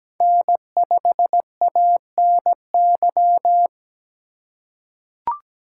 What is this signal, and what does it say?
Morse code, heard as a keyed tone, about 17 wpm, 700 Hz. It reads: N5ANY
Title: USA Call Signs - 5 Characters - 2x Character Spacing - 17wpm